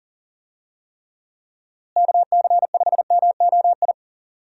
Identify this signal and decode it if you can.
Morse code, heard as a keyed tone, about 40 wpm, 700 Hz. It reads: KC5MOI
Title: USA Call Signs - 6 Characters - Repeated 3x - 40wpm